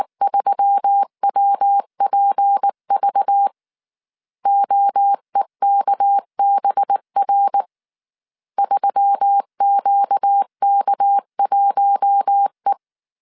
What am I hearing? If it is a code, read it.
Morse code, heard as a keyed tone, about 19 wpm, 800 Hz. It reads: 3WPV OEKBR 3QK1E
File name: CW / DET / 1 6 1500C S C 18wpm 2356